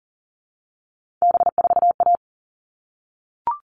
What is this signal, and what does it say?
Morse code, heard as a keyed tone, about 40 wpm, 700 Hz. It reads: B4A
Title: All Cut Numbers - 3-Digit 40wpm